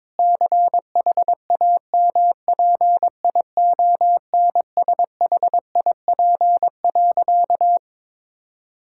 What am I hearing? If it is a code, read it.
Morse code, heard as a keyed tone, about 23 wpm, 700 Hz. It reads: CHAMPIONSHIP.
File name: practice / 22wpm / Sentences with 8-Word Phrases - 22wpm